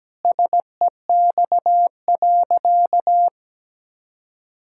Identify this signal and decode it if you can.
Morse code, heard as a keyed tone, about 17 wpm, 700 Hz. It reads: SEX.
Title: Sentences from Top 1600 Words 17wpm